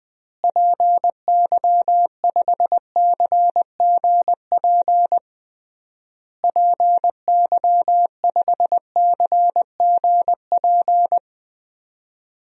Morse code, heard as a keyed tone, about 20 wpm, 700 Hz. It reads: PY5CGP PY5CGP
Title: DXCC-100 Call Signs - 6 Characters - Repeated 3x - 20wpm